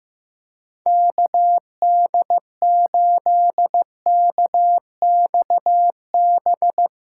Morse code, heard as a keyed tone, about 15 wpm, 700 Hz. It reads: KD8KXB
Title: USA Callsign DE Callsign 15wpm